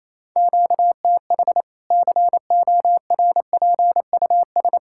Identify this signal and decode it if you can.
Morse code, heard as a keyed tone, about 28 wpm, 700 Hz. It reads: QTH CORPUS
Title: K1USN SST QSO Practice 28wpm